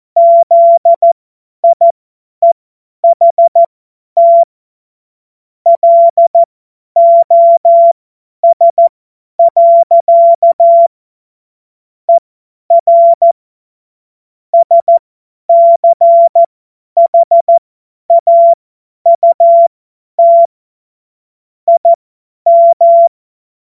Morse code, heard as a keyed tone, about 14 wpm, spaced as for 10 wpm, 680 Hz. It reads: ZIEHT LOS. ER SCHAUT IM
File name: Morsekurs / Witze / Witz 02. Dezember 2025 50 F70